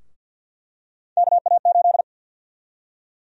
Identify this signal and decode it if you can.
Morse code, heard as a keyed tone, about 49 wpm, 700 Hz. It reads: KA8
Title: USA Call Sign Prefixes - Repeated 2x - 50wpm